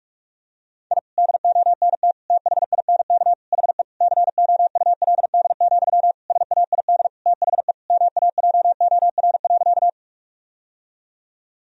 Morse code, heard as a keyed tone, about 46 wpm, 700 Hz. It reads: I DONT THINK HE COULD, SAID THE MAJOR.
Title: Sentences from Top 1000 Words - Review Part 4 - 45wpm